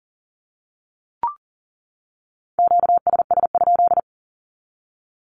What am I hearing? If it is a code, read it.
Morse code, heard as a keyed tone, about 40 wpm, 700 Hz. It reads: QSS?
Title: Q-Codes 40wpm